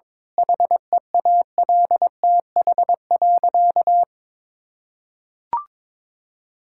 Morse code, heard as a keyed tone, about 21 wpm, 700 Hz. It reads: HEALTH.
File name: Difficult Sentences with Hints - 22wpm